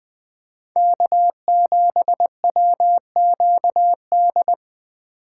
Morse code, heard as a keyed tone, about 20 wpm, 700 Hz. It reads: K7WQD
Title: USA Call Signs - 5 Characters - 8x Character Spacing - 20wpm